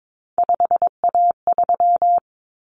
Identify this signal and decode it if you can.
Morse code, heard as a keyed tone, about 22 wpm, 700 Hz. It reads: 5A3
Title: Common Cut Numbers - 3-Digit 22wpm